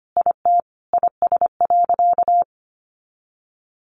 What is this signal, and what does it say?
Morse code, heard as a keyed tone, about 25 wpm, 700 Hz. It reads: IT IS.
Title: Sentences from Top 200 Words 25wpm